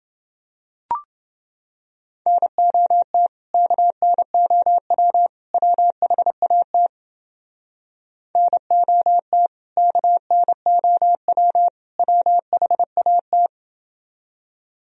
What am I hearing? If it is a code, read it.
Morse code, heard as a keyed tone, about 29 wpm, 700 Hz. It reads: NOT KNOW WHAT NOT KNOW WHAT